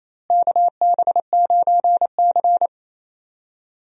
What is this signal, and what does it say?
Morse code, heard as a keyed tone, about 28 wpm, 700 Hz. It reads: KB9C
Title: USA Call Signs - 4 Characters - Repeated 3x - 28wpm